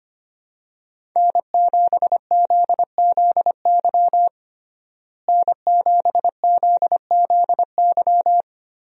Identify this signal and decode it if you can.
Morse code, heard as a keyed tone, about 25 wpm, 700 Hz. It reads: N7ZZY N7ZZY
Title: USA Call Signs - 5 Characters - Repeated 3x - 25wpm